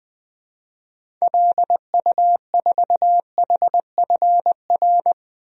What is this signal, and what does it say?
Morse code, heard as a keyed tone, about 20 wpm, 700 Hz. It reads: LU4HFR
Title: DXCC-100 Call Signs - 6 Characters - 20wpm